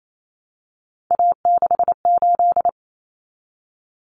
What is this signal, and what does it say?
Morse code, heard as a keyed tone, about 28 wpm, 700 Hz. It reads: A68
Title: Common Cut Numbers - 3-Digit 28wpm